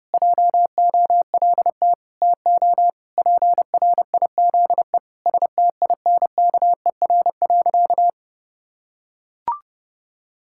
Morse code, heard as a keyed tone, about 30 wpm, 700 Hz. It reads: JOLT TO PRIZE STINKER.
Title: Pangram Challenge v2 - 30wpm